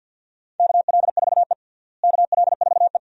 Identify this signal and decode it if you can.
Morse code, heard as a keyed tone, about 47 wpm, 700 Hz. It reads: KL4E KL4E
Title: USA Call Signs - 4 Characters - Repeated 2x - 50wpm